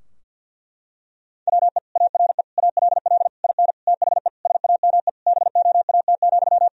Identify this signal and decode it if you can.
Morse code, heard as a keyed tone, about 50 wpm, 700 Hz. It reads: WE ARE ALL IN THE SAME BOAT,
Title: Sentences from Top 1000 Words - Review Part 3 - 50wpm